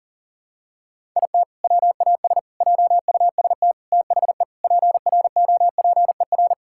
Morse code, heard as a keyed tone, about 40 wpm, 700 Hz. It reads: IT WAS JUST THE PROPER
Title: Sentences from Top 1000 Words - Review Part 5 - 40wpm